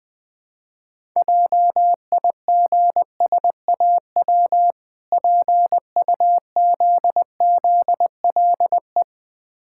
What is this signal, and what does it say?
Morse code, heard as a keyed tone, about 21 wpm, 700 Hz. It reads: JIGSAW PUZZLE
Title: Difficult Words with Hints - 20wpm